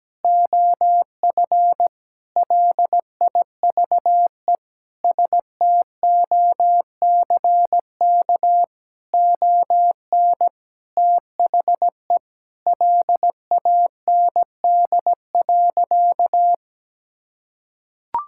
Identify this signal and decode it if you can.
Morse code, heard as a keyed tone, about 17 wpm, 700 Hz. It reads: OF LIVE STOCK ON THE LAND.
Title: Sentences from Top 1000 Words - Review Part 5 - 17wpm